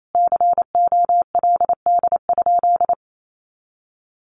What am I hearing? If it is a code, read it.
Morse code, heard as a keyed tone, about 28 wpm, 700 Hz. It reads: COLD?
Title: A Sound Of Thunder 28wpm